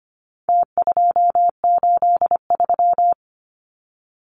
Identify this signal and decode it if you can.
Morse code, heard as a keyed tone, about 25 wpm, 700 Hz. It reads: T283
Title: Numbers - Contest Style - 4-Digit 25wpm